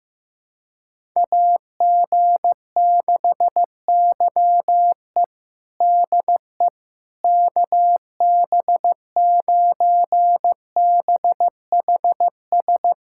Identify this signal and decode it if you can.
Morse code, heard as a keyed tone, about 15 wpm, 700 Hz. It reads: AG6YE DE KB9BHS
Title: USA Callsign DE Callsign 15wpm